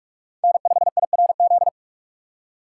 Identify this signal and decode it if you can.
Morse code, heard as a keyed tone, about 43 wpm, 700 Hz. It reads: N5IRZ
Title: USA Call Signs - 4x Character Spacing - 45wpm